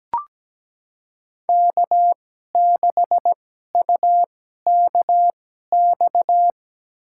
Morse code, heard as a keyed tone, about 17 wpm, spaced as for 12 wpm, 700 Hz. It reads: K6UKX
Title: USA Call Signs - 5 Characters - 2x Character Spacing - 17wpm